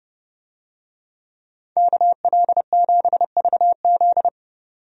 Morse code, heard as a keyed tone, about 30 wpm, 700 Hz. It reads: KL7VZ
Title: DXCC-100 Call Signs - 5 Characters - 30wpm